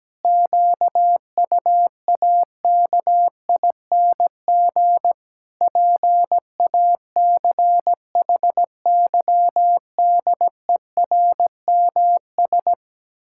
Morse code, heard as a keyed tone, about 17 wpm, 700 Hz. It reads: QUAKING PACHYDERMS
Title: Pangram Challenge v2 - 17wpm